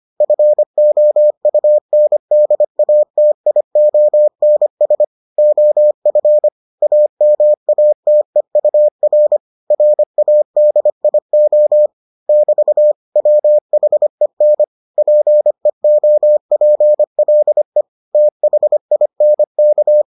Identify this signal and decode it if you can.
Morse code, heard as a keyed tone, about 25 wpm, 600 Hz. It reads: FOUNDATIONS OF AMATEUR RADIO <BT> WHEN PEOPLE THINK